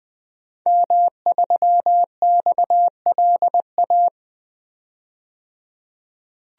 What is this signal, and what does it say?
Morse code, heard as a keyed tone, about 20 wpm, 700 Hz. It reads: M3XLA
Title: DXCC-100 Call Signs - 5 Characters - 20wpm